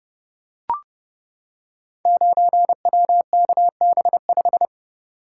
Morse code, heard as a keyed tone, about 30 wpm, 700 Hz. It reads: 9WKB5
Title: Code Groups - 5 Letter:Numbers - 30wpm